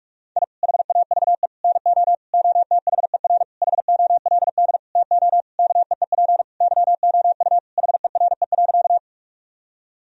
Morse code, heard as a keyed tone, about 42 wpm, 700 Hz. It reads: I HAVE NO OTHER HOLD TO KEEP YOU HERE.
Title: Sentences from Top 1000 Words - Review Part 2 - 45wpm